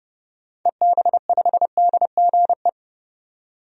Morse code, heard as a keyed tone, about 30 wpm, 700 Hz. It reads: EB5DGE
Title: DXCC-100 Call Signs - 6 Characters - 30wpm